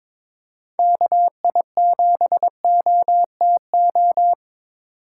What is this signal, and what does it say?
Morse code, heard as a keyed tone, about 22 wpm, 700 Hz. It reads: KI7OTO
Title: USA Call Signs - 6 Characters - 3x Character Spacing - 22wpm